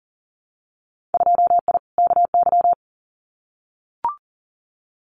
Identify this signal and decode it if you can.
Morse code, heard as a keyed tone, about 37 wpm, 700 Hz. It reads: 2I KY
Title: Winter Field Day Exchanges 40wpm